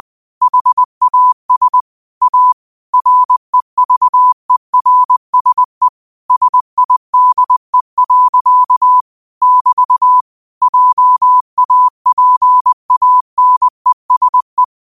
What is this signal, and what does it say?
Morse code, heard as a keyed tone, about 20 wpm, 1000 Hz. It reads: HAS A REVERSE SIDE. <BT> JAPANESE
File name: 20wpm